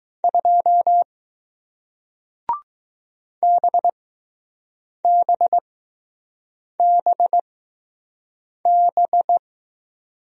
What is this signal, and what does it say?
Morse code, heard as a keyed tone, about 20 wpm, spaced as for 6 wpm, 700 Hz. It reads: 2 BBBB